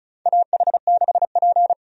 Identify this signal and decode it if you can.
Morse code, heard as a keyed tone, about 34 wpm, 700 Hz. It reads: AH6P
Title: USA Call Signs - 4 Characters - 35wpm